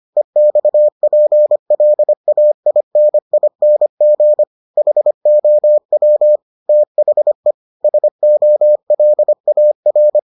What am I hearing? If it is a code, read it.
Morse code, heard as a keyed tone, about 25 wpm, 600 Hz. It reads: EXPLAINING HOW THE SOLAR